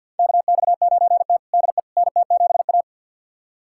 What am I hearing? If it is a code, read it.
Morse code, heard as a keyed tone, about 49 wpm, 700 Hz. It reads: KX9T DE NT7A
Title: USA Callsign DE Callsign - 4 Characters - 50wpm